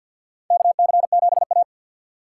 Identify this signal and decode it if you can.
Morse code, heard as a keyed tone, about 52 wpm, 700 Hz. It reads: KC7A
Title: USA Call Signs - 4 Characters - 6x Character Spacing - 50wpm